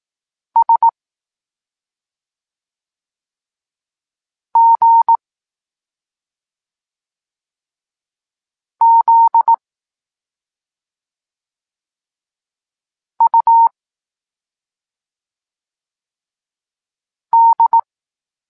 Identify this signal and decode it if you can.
Morse code, heard as a keyed tone, about 18 wpm, 920 Hz. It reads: S G Z U D
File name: Tekst-26 1